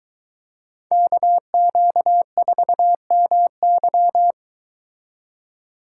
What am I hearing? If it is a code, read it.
Morse code, heard as a keyed tone, about 22 wpm, 700 Hz. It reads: KQ4MY